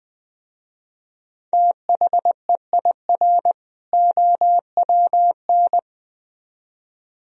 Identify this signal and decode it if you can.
Morse code, heard as a keyed tone, about 20 wpm, 700 Hz. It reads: THEIR OWN